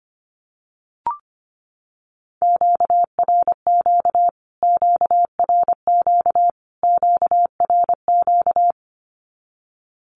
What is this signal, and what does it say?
Morse code, heard as a keyed tone, about 25 wpm, 700 Hz. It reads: QRQ QRQ QRQ
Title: Q-Codes Repeated 3x 25wpm